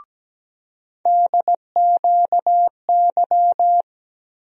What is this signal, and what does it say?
Morse code, heard as a keyed tone, about 17 wpm, 700 Hz. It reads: DQY
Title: USA Call Sign Postfixes - 3 Letters 17wpm